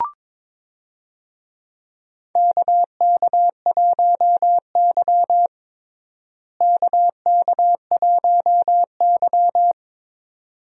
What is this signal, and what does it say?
Morse code, heard as a keyed tone, about 22 wpm, 700 Hz. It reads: KK1Y KK1Y
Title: USA Call Signs - 4 Characters - Repeated 2x - 22wpm